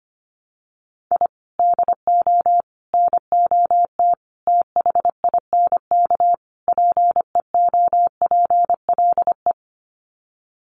Morse code, heard as a keyed tone, about 25 wpm, 700 Hz. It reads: I DO NOT THINK PEOPLE
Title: Sets of 5 Words Encore2 25wpm